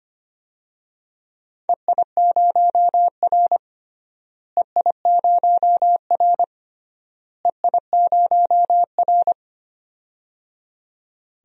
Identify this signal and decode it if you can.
Morse code, heard as a keyed tone, about 25 wpm, 700 Hz. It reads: EI0R EI0R EI0R
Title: DXCC-100 Call Signs - 4 Characters - Repeated 3x - 25wpm